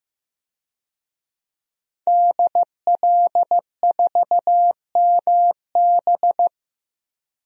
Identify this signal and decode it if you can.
Morse code, heard as a keyed tone, about 15 wpm, 700 Hz. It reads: DL4MB
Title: DXCC-100 Call Signs - 5 Characters - 15wpm